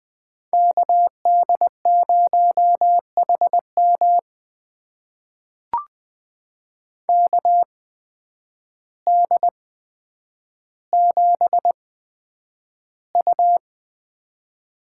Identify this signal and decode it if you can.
Morse code, heard as a keyed tone, about 20 wpm, spaced as for 9 wpm, 700 Hz. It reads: KD0HM KD7U
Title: USA Call Signs - 5 Characters - 8x Character Spacing - 20wpm